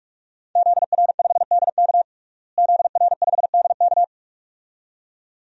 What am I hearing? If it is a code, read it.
Morse code, heard as a keyed tone, about 47 wpm, 700 Hz. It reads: ZR5DK ZR5DK
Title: DXCC-100 Call Signs - 5 Characters - Repeated 2x - 45wpm